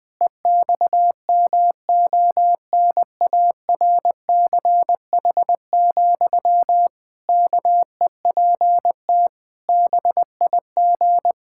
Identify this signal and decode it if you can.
Morse code, heard as a keyed tone, about 20 wpm, 700 Hz. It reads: EXMONARCH, KEPT BIG
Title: Pangram Challenge v2 - 20wpm